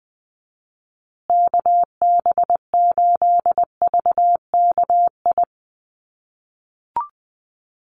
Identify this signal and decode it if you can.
Morse code, heard as a keyed tone, about 20 wpm, 700 Hz. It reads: KB8VKI
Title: USA Call Signs Repeated 3x 20wpm